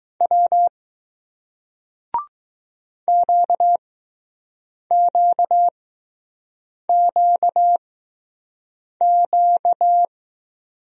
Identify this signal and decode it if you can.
Morse code, heard as a keyed tone, about 20 wpm, 700 Hz. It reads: W Q Q Q Q